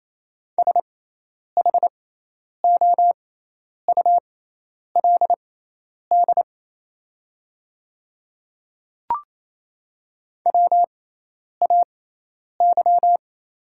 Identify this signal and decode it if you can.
Morse code, heard as a keyed tone, about 28 wpm, spaced as for 10 wpm, 700 Hz. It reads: SHOULD WAY
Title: 6x Character Spacing - Top 500 Words - 28wpm